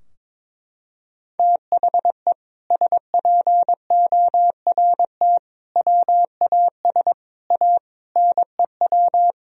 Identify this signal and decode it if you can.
Morse code, heard as a keyed tone, about 22 wpm, 700 Hz. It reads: THE SPORT WAS A NEW
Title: Sentences from Top 1000 Words - Review Part 4 - 22wpm